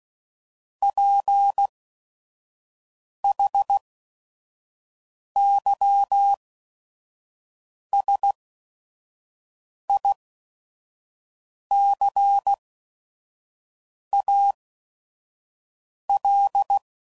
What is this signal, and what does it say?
Morse code, heard as a keyed tone, about 16 wpm, spaced as for 5 wpm, 800 Hz. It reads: PHYSICAL
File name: CW Code Groups @5WPM / Technical Subjects / SwPwrSup